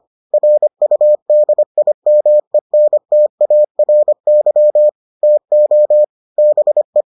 Morse code, heard as a keyed tone, about 25 wpm, 600 Hz. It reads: RUDIMENTARY TO BE